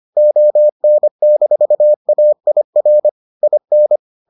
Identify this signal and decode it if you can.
Morse code, heard as a keyed tone, about 25 wpm, 600 Hz. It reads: ON-AIR IN